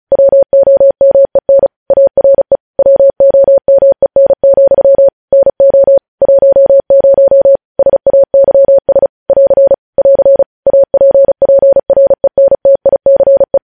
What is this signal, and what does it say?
Morse code, heard as a keyed tone, about 36 wpm, 550 Hz. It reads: WOMEN ARE WOMEN, NO 10 SAYS <AR> <AR> APPRENTICE
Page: news headlines in morse code at 35 wpm